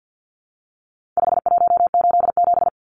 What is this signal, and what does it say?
Morse code, heard as a keyed tone, about 48 wpm, 700 Hz. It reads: 5187
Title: Numbers - 4-Digit 50wpm